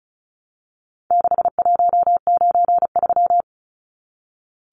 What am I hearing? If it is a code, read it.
Morse code, heard as a keyed tone, about 34 wpm, 700 Hz. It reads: 6193
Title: Numbers - 4-Digit 35wpm